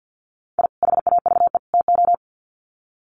Morse code, heard as a keyed tone, about 50 wpm, 700 Hz. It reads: I HAVE TO
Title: Sets of 3 Words 50wpm